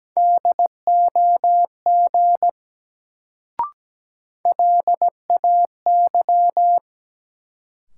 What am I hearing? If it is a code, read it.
Morse code, heard as a keyed tone, about 17 wpm, 700 Hz. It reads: DOG LAY
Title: Sentences from Top 400 Words - New Word Drill - 17wpm